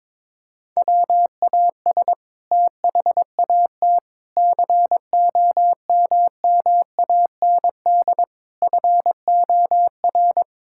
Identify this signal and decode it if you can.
Morse code, heard as a keyed tone, about 21 wpm, 700 Hz. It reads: WAS THAT COMMAND FOR
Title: Sentences from Top 1600 Words 22wpm